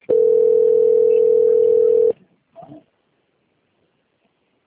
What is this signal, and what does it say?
Ringback tone (North America), 440 + 480 Hz.